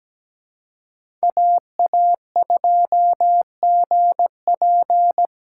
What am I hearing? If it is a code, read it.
Morse code, heard as a keyed tone, about 17 wpm, 700 Hz. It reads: AA2GP
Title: USA Call Signs - 5 Characters - 4x Character Spacing - 17wpm